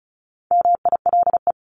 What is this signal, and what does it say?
Morse code, heard as a keyed tone, about 36 wpm, 700 Hz. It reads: MILE
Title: Top 300 Words 35wpm